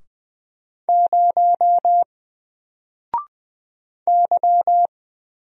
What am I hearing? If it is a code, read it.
Morse code, heard as a keyed tone, about 20 wpm, 700 Hz. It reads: 0 Y
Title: Single Letter-Number Alt 20wpm